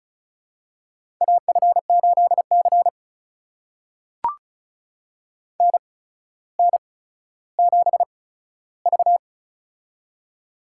Morse code, heard as a keyed tone, about 36 wpm, spaced as for 15 wpm, 700 Hz. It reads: AF8C NN7V
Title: USA Call Signs - 4 Characters - 8x Character Spacing - 35wpm